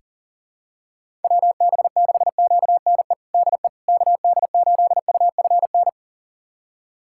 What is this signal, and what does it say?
Morse code, heard as a keyed tone, about 37 wpm, 700 Hz. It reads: WB6QNE DE KD8UFN
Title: USA Callsign DE Callsign - 6 Characters - 40wpm